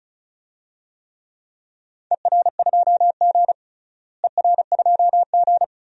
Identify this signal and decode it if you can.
Morse code, heard as a keyed tone, about 34 wpm, 700 Hz. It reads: ER2G ER2G
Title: DXCC-100 Call Signs - 4 Characters - Repeated 3x - 35wpm